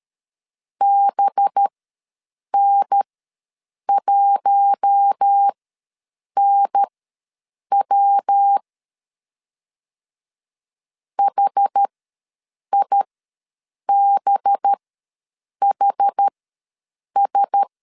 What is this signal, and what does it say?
Morse code, heard as a keyed tone, about 13 wpm, spaced as for 7 wpm, 800 Hz. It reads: BN1NW HIBHS
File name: 5 LB ws 12wpm